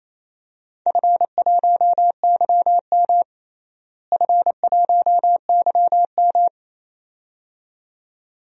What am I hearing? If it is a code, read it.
Morse code, heard as a keyed tone, about 28 wpm, 700 Hz. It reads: F1YM F1YM
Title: DXCC-100 Call Signs - 4 Characters - Repeated 2x - 28wpm